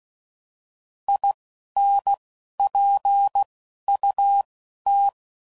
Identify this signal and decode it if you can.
Morse code, heard as a keyed tone, about 16 wpm, spaced as for 11 wpm, 800 Hz. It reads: INPUT
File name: CW Code Groups @10WPM/Technical Subjects